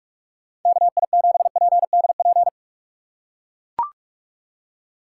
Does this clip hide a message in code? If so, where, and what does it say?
Yes: Morse code from 0.7 to 2.5 s, heard as a keyed tone, about 41 wpm, 700 Hz. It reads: KI7PDP